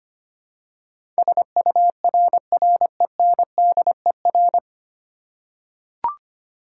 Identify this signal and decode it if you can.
Morse code, heard as a keyed tone, about 25 wpm, 700 Hz. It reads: SURRENDER
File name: practice / 25wpm / 9 Letter Words - First 3 Letters Given - 25wpm